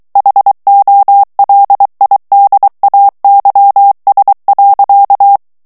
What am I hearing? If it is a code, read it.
Morse code, heard as a keyed tone, about 23 wpm, 800 Hz. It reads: HOLIDAYS.
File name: morse mp3 / ANGLAIS texte clair 1 vitesse 22 mots minute